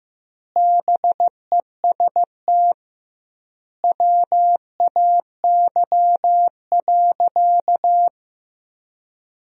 Morse code, heard as a keyed tone, about 16 wpm, 700 Hz. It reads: BEST WAY.
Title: 2x Word-Spacing - Sentences from Top 400 Words - 15wpm